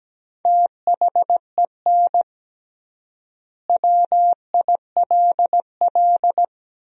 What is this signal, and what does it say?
Morse code, heard as a keyed tone, about 17 wpm, 700 Hz. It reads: THEN WILL
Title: 3x Word-Spacing - Sentences from Top 200 Words - 17wpm